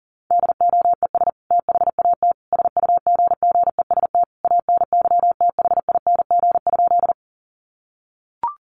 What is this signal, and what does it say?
Morse code, heard as a keyed tone, about 40 wpm, 700 Hz. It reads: DOES THAT SUGGEST ANYTHING?
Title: Sentences from Top 1300 Words 40wpm